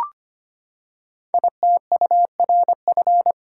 Morse code, heard as a keyed tone, about 25 wpm, 700 Hz. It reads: ITURF
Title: Code Groups - 5 Letters - 25wpm